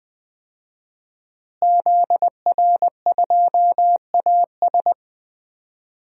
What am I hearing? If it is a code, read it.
Morse code, heard as a keyed tone, about 20 wpm, 700 Hz. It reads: ZR2AS
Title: DXCC-100 Call Signs - 5 Characters - Repeated 3x - 20wpm